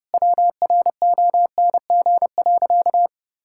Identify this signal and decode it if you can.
Morse code, heard as a keyed tone, about 30 wpm, 700 Hz. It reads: WRONG.
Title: Sentences from Top 1000 Words - Review Part 6 - 30wpm